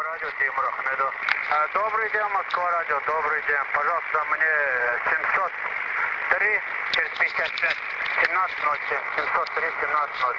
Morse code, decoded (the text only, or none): none